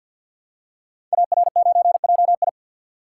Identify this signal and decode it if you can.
Morse code, heard as a keyed tone, about 48 wpm, 700 Hz. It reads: AR9JI
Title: Code Groups - 5 Letter:Numbers - 50wpm